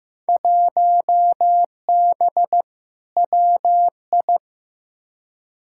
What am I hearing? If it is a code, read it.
Morse code, heard as a keyed tone, about 15 wpm, 700 Hz. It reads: 1B WI
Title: Field Day Exchanges Encore - 15wpm